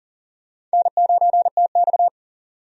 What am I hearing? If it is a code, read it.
Morse code, heard as a keyed tone, about 39 wpm, 700 Hz. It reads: N9TX
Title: USA Call Signs - 4 Characters - 40wpm